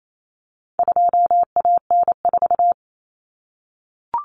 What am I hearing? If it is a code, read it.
Morse code, heard as a keyed tone, about 28 wpm, 700 Hz. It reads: 2AN4
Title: Common Cut Numbers - 4-Digit 28wpm